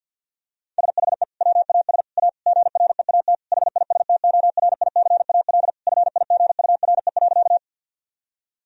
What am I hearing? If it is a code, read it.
Morse code, heard as a keyed tone, about 50 wpm, 700 Hz. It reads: SHE WAS A GREAT HISTORICAL FIGURE.
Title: Sentences from Top 1000 Words - Review Part 4 - 50wpm